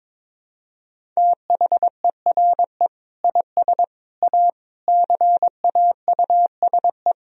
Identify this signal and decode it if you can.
Morse code, heard as a keyed tone, about 22 wpm, 700 Hz. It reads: THERE IS A CAUSE
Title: Sentences from Top 1000 Words - Review Part 1 - 22wpm